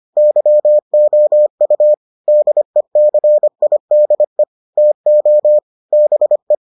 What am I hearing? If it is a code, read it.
Morse code, heard as a keyed tone, about 25 wpm, 600 Hz. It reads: YOU DECIDE TO BE